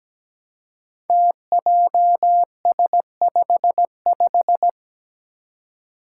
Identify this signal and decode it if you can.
Morse code, heard as a keyed tone, about 17 wpm, 700 Hz. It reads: TJS55
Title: Five Challenging Letter-Number Combinations 17wpm